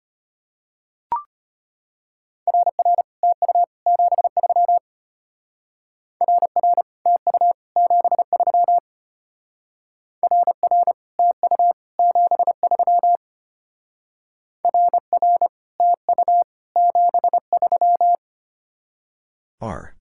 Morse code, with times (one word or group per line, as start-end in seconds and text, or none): 2.5-3.0 RR
3.2-3.6 TU
3.9-4.8 73
6.2-6.8 RR
7.1-7.5 TU
7.8-8.8 73
10.2-10.9 RR
11.2-11.7 TU
12.0-13.2 73
14.7-15.5 RR
15.8-16.4 TU
16.8-18.1 73